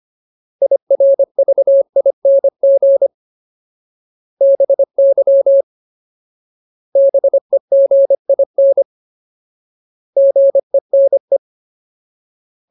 Morse code, heard as a keyed tone, about 25 wpm, 550 Hz. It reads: IRVING BY BEGIN GENE
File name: Ultimate Word List / 25WPM Ultimate